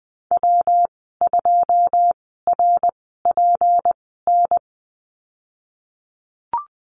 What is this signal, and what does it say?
Morse code, heard as a keyed tone, about 20 wpm, spaced as for 15 wpm, 700 Hz. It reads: W2RPN
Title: 173 Lesson 28g - Number 9 - Call Signs 20wpm